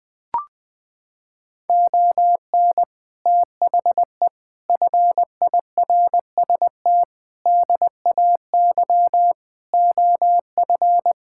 Morse code, heard as a keyed tone, about 20 wpm, 700 Hz. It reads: ON THE FIRST DAY OF
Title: Sentences with 8-Word Phrases - 20wpm